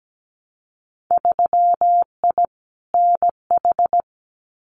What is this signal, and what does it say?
Morse code, heard as a keyed tone, about 17 wpm, 700 Hz. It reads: 3I NH
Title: Winter Field Day Exchanges Encore 17wpm